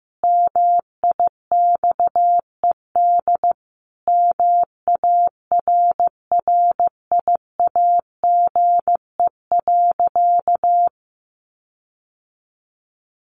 Morse code, heard as a keyed tone, about 15 wpm, 700 Hz. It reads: MIXED MARRIAGE.
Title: Sentences from Top 1500 Words 15wpm